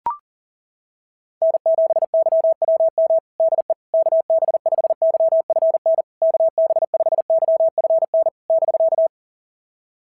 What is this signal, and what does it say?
Morse code, heard as a keyed tone, about 39 wpm, 660 Hz. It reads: N7YWM DE KB5YFN KB5YFN <BK>
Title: SOTA QSO Practice 40wpm